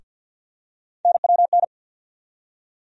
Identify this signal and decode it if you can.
Morse code, heard as a keyed tone, about 52 wpm, 700 Hz. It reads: NWN
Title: USA Call Sign Postfixes - 3 Letters 50wpm